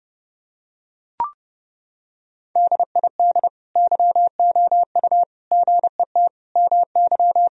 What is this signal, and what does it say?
Morse code, heard as a keyed tone, about 32 wpm, 700 Hz. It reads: DID YOU GET MY
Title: Sentences from Top 1000 Words - Review Part 2 - 30wpm